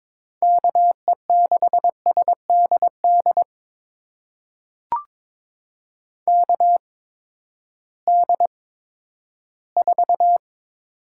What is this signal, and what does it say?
Morse code, heard as a keyed tone, about 22 wpm, spaced as for 13 wpm, 700 Hz. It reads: KE6SDD KD4
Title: USA Call Signs - 6 Characters - 8x Character Spacing - 22wpm